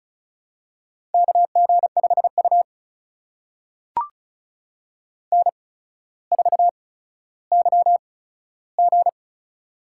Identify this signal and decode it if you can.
Morse code, heard as a keyed tone, about 36 wpm, spaced as for 15 wpm, 700 Hz. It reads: KG5U N4YG
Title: USA Call Signs - 4 Characters - 8x Character Spacing - 35wpm